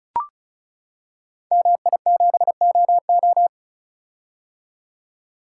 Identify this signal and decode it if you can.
Morse code, heard as a keyed tone, about 36 wpm, 700 Hz. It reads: MI7OO
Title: Code Groups - 5 Letter:Numbers - 35wpm